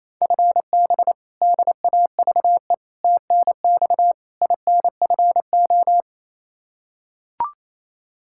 Morse code, heard as a keyed tone, about 28 wpm, 700 Hz. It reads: FB DAVE TNX INFO
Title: K1USN SST QSO Practice 28wpm